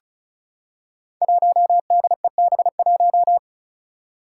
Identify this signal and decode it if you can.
Morse code, heard as a keyed tone, about 34 wpm, 700 Hz. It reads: 1DEB1
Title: Five Challenging Letter-Number Combinations 35wpm